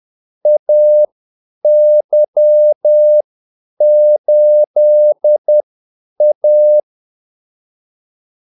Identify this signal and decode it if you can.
Morse code, heard as a keyed tone, about 10 wpm, spaced as for 8 wpm, 600 Hz. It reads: AY8A
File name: Callsigns-10-8wpm